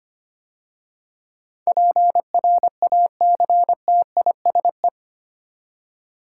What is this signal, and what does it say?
Morse code, heard as a keyed tone, about 25 wpm, 700 Hz. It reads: PRACTISE
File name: 8 Letter Words - First 3 Letters Given - 25wpm